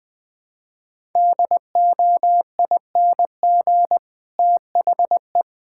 Morse code, heard as a keyed tone, about 20 wpm, 700 Hz. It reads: DOING THE
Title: Sentences with 7-Word Phrases - Spoken Beginning - 20wpm